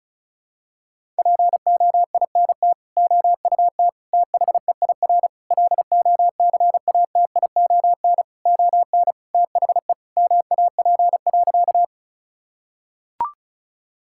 Morse code, heard as a keyed tone, about 34 wpm, 700 Hz. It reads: POINT OUT THEIR LOCATION ON THE MAP.
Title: Sentences from Top 1000 Words - Review Part 6 - 35wpm